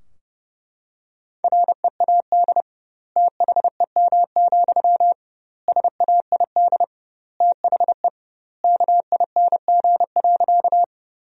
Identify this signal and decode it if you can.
Morse code, heard as a keyed tone, about 32 wpm, 700 Hz. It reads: READ THEM, SAID THE KING.
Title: 2x Word-Spacing - Sentences from Top 400 Words - 30wpm